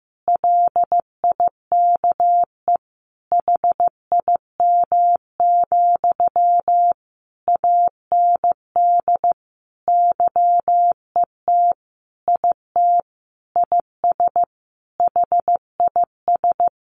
Morse code, heard as a keyed tone, about 14 wpm, 700 Hz. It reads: LIKE HIM, AND YET IT IS HIS
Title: Sentences from Top 800 Words 15wpm